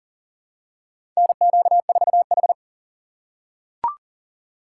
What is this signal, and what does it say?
Morse code, heard as a keyed tone, about 40 wpm, 700 Hz. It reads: NQ4H
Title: USA Call Signs - 4 Characters - 6x Character Spacing - 40wpm